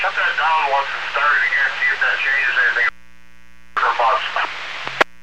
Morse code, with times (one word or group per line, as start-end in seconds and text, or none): none